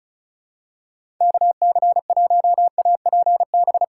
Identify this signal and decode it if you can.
Morse code, heard as a keyed tone, about 36 wpm, 700 Hz. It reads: KC1APB
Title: USA Call Signs - 2x Character Spacing - 35wpm